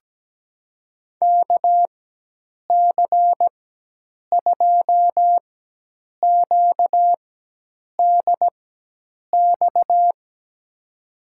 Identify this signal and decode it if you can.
Morse code, heard as a keyed tone, about 17 wpm, spaced as for 8 wpm, 700 Hz. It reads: KC2QDX
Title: USA Call Signs - 4x Character Spacing - 17wpm